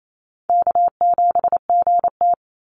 Morse code, heard as a keyed tone, about 28 wpm, 700 Hz. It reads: K7GT
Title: NA SOTA Chaser Callsigns 28wpm